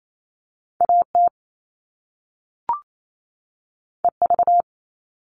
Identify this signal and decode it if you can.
Morse code, heard as a keyed tone, about 28 wpm, 700 Hz. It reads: AT EV
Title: All Cut Numbers - 2-Digit 28wpm